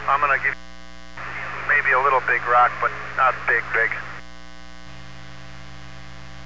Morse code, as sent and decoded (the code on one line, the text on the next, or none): none